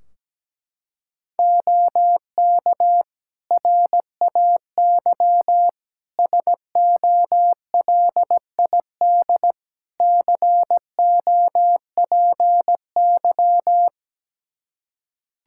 Morse code, heard as a keyed tone, about 17 wpm, 700 Hz. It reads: OK RAY SOLID COPY
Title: Practice QSO - Exchange 3 - More Info - 17wpm